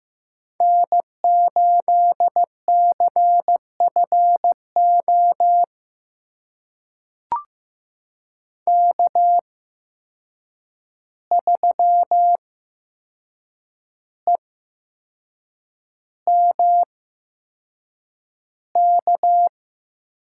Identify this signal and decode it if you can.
Morse code, heard as a keyed tone, about 15 wpm, spaced as for 6 wpm, 700 Hz. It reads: N8CFO K3EMK
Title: USA Call Signs - 5 Characters - 8x Character Spacing - 15wpm